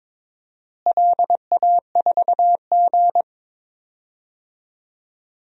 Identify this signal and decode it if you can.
Morse code, heard as a keyed tone, about 22 wpm, 700 Hz. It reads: LA4G
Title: DXCC-100 Call Signs - 4 Characters - Repeated 2x - 22wpm